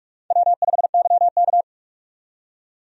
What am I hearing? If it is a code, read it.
Morse code, heard as a keyed tone, about 45 wpm, 700 Hz. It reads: W5YK
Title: USA Call Signs - 4 Characters - 45wpm